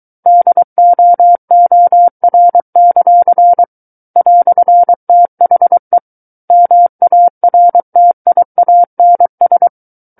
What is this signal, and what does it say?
Morse code, heard as a keyed tone, about 23 wpm, 700 Hz. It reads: DOOR; "THE MARTIANS
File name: Book One / 14-Chapter 14 - 23 WPM